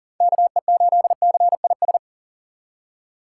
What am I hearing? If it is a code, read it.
Morse code, heard as a keyed tone, about 40 wpm, 700 Hz. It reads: KE8CIS
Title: USA Call Signs - 8x Character Spacing - 40wpm